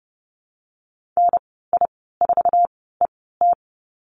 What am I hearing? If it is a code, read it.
Morse code, heard as a keyed tone, about 30 wpm, spaced as for 17 wpm, 700 Hz. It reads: NI4ET
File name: Lesson 15g - Number 5 - Call Signs 30wpm